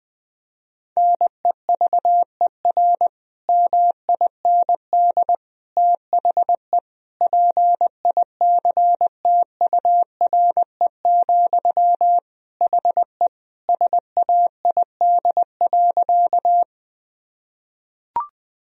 Morse code, heard as a keyed tone, about 20 wpm, 700 Hz. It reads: NEVER MIND THE PICTURE, HE SAID.
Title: Sentences from Top 1000 Words - Review Part 1 - 20wpm